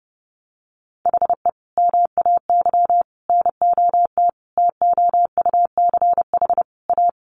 Morse code, heard as a keyed tone, about 30 wpm, 700 Hz. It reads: HE MAY NOT TOUCH A
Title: Sentences from Top 1500 Words 30wpm